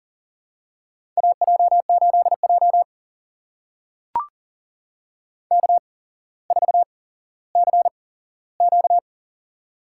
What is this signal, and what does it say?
Morse code, heard as a keyed tone, about 43 wpm, spaced as for 17 wpm, 700 Hz. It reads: AJ8J K4CQ
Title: USA Call Signs - 4 Characters - 8x Character Spacing - 40wpm